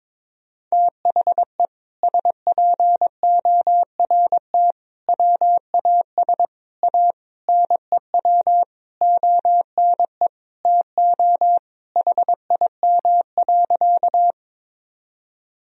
Morse code, heard as a keyed tone, about 22 wpm, 700 Hz. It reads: THE SPORT WAS A NEW ONE TO HIM.
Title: Sentences from Top 1000 Words - Review Part 4 - 22wpm